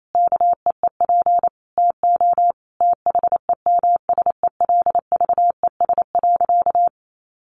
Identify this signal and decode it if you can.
Morse code, heard as a keyed tone, about 28 wpm, 700 Hz. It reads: KEEP TO THEMSELVES.
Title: Sentences from Top 1300 Words 28wpm